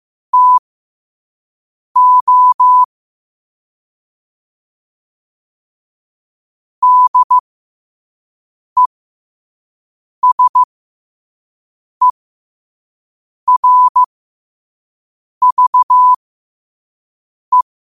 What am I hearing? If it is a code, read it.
Morse code, heard as a keyed tone, about 15 wpm, spaced as for 5 wpm, 1000 Hz. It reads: TO DESERVE
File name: qotdfiles / 5wpm